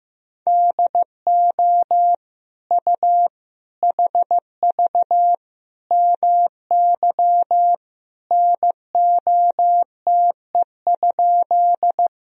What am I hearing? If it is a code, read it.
Morse code, heard as a keyed tone, about 15 wpm, 700 Hz. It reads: DO U HV MY NOTE?
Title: Sentences from Top 500 Words and Abbreviations 15wpm